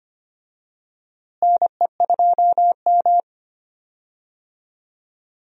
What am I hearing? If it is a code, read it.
Morse code, heard as a keyed tone, about 25 wpm, 700 Hz. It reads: NE2M
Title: USA Call Signs - 4 Characters - Repeated 2x - 25wpm